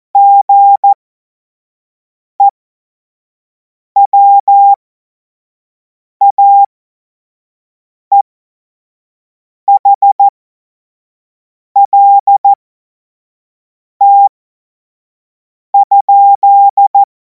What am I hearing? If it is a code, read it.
Morse code, heard as a keyed tone, about 14 wpm, spaced as for 5 wpm, 800 Hz. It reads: GEWAEHLT?
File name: Morsekurs / Witze / Witz 16. Juni 2026 25 F70